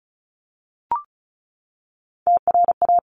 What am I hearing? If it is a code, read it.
Morse code, heard as a keyed tone, about 36 wpm, 700 Hz. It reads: TRA
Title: Top Three Letter Combinations in Words 35wpm